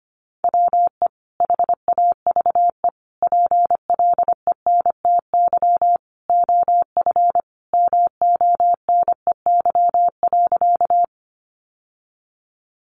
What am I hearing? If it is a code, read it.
Morse code, heard as a keyed tone, about 25 wpm, 700 Hz. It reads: WE HAVE PLENTY OF MONEY.
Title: Sentences from Top 1300 Words 25wpm